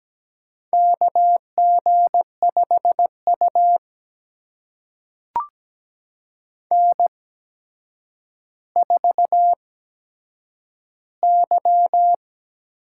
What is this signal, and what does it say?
Morse code, heard as a keyed tone, about 17 wpm, spaced as for 8 wpm, 700 Hz. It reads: KG5U N4Y
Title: USA Call Signs - 4 Characters - 8x Character Spacing - 17wpm